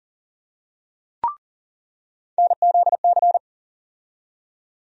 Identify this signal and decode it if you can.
Morse code, heard as a keyed tone, about 40 wpm, 700 Hz. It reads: NZC
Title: USA Call Sign Postfixes - 3 Letters 40wpm